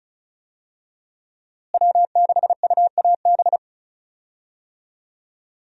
Morse code, heard as a keyed tone, about 34 wpm, 700 Hz. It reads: W6UAB
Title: USA Call Signs - 5 Characters - Repeated 2x - 35wpm